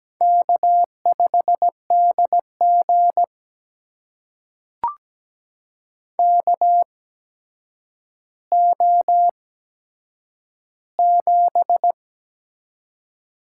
Morse code, heard as a keyed tone, about 17 wpm, spaced as for 8 wpm, 700 Hz. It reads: K5DG KO7
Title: USA Call Signs - 4 Characters - 8x Character Spacing - 17wpm